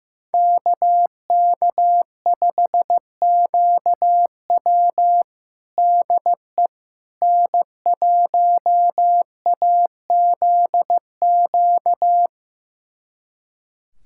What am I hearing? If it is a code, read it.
Morse code, heard as a keyed tone, about 15 wpm, 700 Hz. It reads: KK5QW DE N1AZQ
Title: USA Callsign DE Callsign - 5 Characters - 15wpm